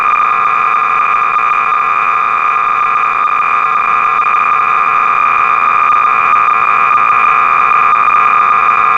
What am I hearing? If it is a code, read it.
Cassette data (Kansas City type).
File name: sorcerer / 300baud / Salvo-(1979)(J.T.Staley)-BASIC-Sorcerer-300baud